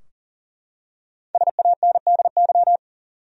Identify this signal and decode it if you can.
Morse code, heard as a keyed tone, about 37 wpm, 700 Hz. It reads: SANDY
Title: USA Common Names - 40wpm